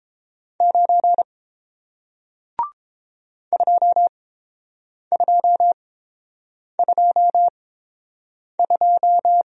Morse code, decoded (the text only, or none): 9 2222